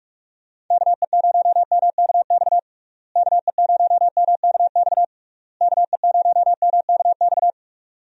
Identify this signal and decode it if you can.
Morse code, heard as a keyed tone, about 41 wpm, 700 Hz. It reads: KE0MKX KE0MKX KE0MKX
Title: USA Call Signs - 6 Characters - Repeated 3x - 45wpm